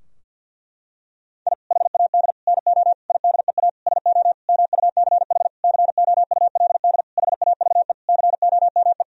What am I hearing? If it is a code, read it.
Morse code, heard as a keyed tone, about 50 wpm, 700 Hz. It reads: I HAD NO IDEA SO MUCH COULD HAVE COME
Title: Sentences from Top 1000 Words - Review Part 4 - 50wpm